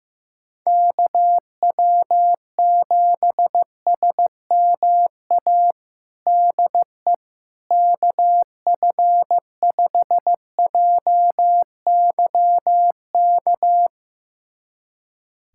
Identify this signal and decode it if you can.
Morse code, heard as a keyed tone, about 16 wpm, 700 Hz. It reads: KW7SMA DE KF5JYK
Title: USA Callsign DE Callsign 15wpm